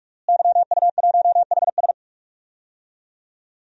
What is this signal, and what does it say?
Morse code, heard as a keyed tone, about 43 wpm, 700 Hz. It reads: YU1HS
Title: DXCC-100 Call Signs - 5 Characters - Repeated 2x - 45wpm